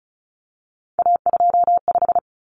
Morse code, heard as a keyed tone, about 34 wpm, 700 Hz. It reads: A25
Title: Common Cut Numbers - 3-Digit 35wpm